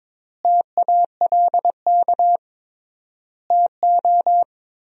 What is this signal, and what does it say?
Morse code, heard as a keyed tone, about 21 wpm, 700 Hz. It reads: TALK TO
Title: 3x Word-Spacing - Sentences from Top 400 Words - 22wpm